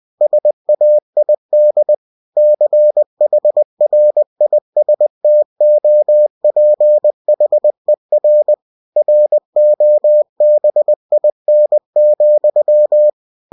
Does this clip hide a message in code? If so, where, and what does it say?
Yes: Morse code from 0.2 to 13.1 s, heard as a keyed tone, about 20 wpm, 600 Hz. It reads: SAID CHRISTOPHER ROBIN,